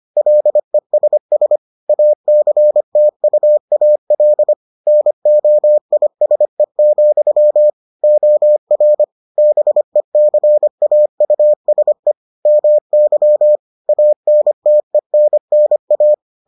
Morse code, heard as a keyed tone, about 25 wpm, 600 Hz. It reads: LESS ACTUAL NOISE, OR BECAUSE MY ANTENNA